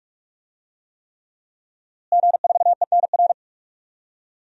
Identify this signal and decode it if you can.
Morse code, heard as a keyed tone, about 46 wpm, 700 Hz. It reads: G4ENR
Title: DXCC-100 Call Signs - 5 Characters - Repeated 3x - 45wpm